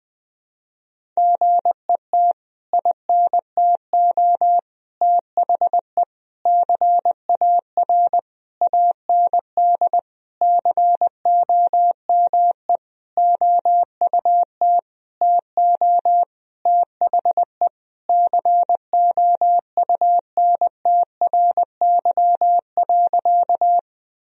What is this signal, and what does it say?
Morse code, heard as a keyed tone, about 19 wpm, 700 Hz. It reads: GET INTO THE CAR AND COME OUT TO THE COUNTRY.
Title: Sentences from Top 1000 Words - Review Part 1 - 20wpm